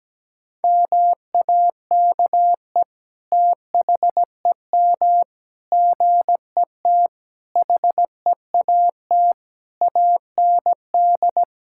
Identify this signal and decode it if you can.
Morse code, heard as a keyed tone, about 17 wpm, 700 Hz. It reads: MAKE THEM GET HEAT AND
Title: Sentences from Top 1600 Words 17wpm